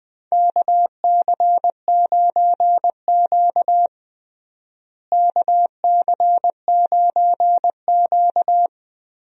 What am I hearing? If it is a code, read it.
Morse code, heard as a keyed tone, about 20 wpm, 700 Hz. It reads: KC9Q KC9Q
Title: USA Call Signs - 4 Characters - Repeated 3x - 20wpm